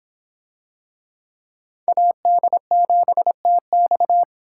Morse code, heard as a keyed tone, about 26 wpm, 700 Hz. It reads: AD7TX